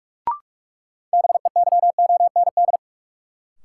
Morse code, heard as a keyed tone, about 42 wpm, 700 Hz. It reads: BEYOND
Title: Sentences from Top 1000 Words - Review Part 6 - New Word Drill - 45wpm